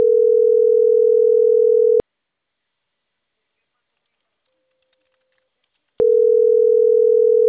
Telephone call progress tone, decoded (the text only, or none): ringback tone